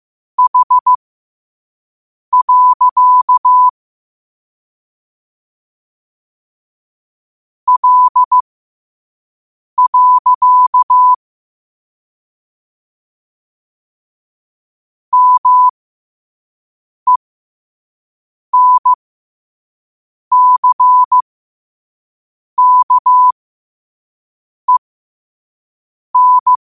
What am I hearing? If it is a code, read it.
Morse code, heard as a keyed tone, about 15 wpm, spaced as for 5 wpm, 1000 Hz. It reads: H. L. MENCKEN
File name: qotdfiles / 5wpm